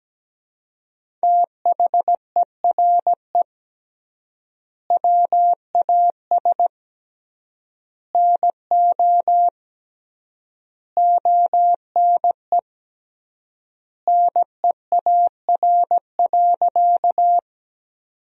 Morse code, heard as a keyed tone, about 17 wpm, 700 Hz. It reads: THERE WAS NO ONE NEAR.
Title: 3x Word-Spacing - Sentences from Top 200 Words - 17wpm